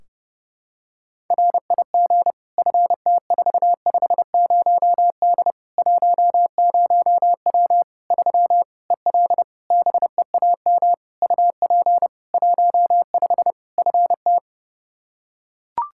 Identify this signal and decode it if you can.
Morse code, heard as a keyed tone, about 32 wpm, 700 Hz. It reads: RIG FT450D 10W 3 EL BEAM UP 15 FT
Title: Practice QSO - Exchange 2 - Info - 30wpm